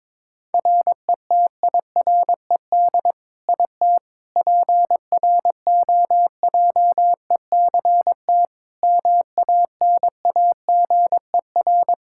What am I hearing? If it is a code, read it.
Morse code, heard as a keyed tone, about 22 wpm, 700 Hz. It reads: RETIRED IT PROJECT MANAGER
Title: Practice QSO - Exchange 3 - More Info - 22wpm